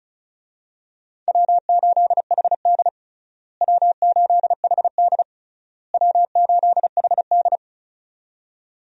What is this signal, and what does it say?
Morse code, heard as a keyed tone, about 34 wpm, 700 Hz. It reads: W8HD W8HD W8HD
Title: USA Call Signs - 4 Characters - Repeated 3x - 35wpm